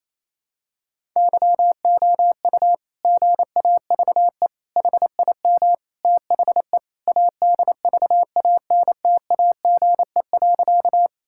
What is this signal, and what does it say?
Morse code, heard as a keyed tone, about 27 wpm, 700 Hz. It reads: YOU GAVE HIM THE ADVANTAGE.
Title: Sentences from Top 1000 Words - Review Part 6 - 28wpm